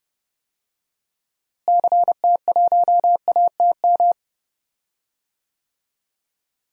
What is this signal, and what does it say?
Morse code, heard as a keyed tone, about 30 wpm, 700 Hz. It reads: CT1ATM
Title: DXCC-100 Call Signs - 6 Characters - Repeated 2x - 30wpm